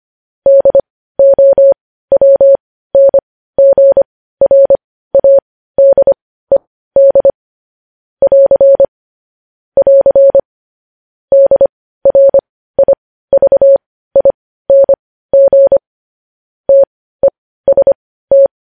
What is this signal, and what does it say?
Morse code, heard as a keyed tone, about 25 wpm, spaced as for 15 wpm, 550 Hz. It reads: DOWNGRADED <AR> <AR> DRIVING TEST
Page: News Headlines in Morse Code at 15 WPM